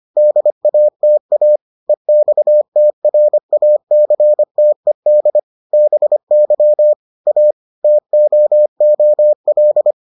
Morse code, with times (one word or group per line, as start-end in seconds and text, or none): 0.2-1.6 DATA
1.9-5.4 EXTRACTED
5.7-6.9 BY
7.3-7.5 A
7.8-9.9 TOOL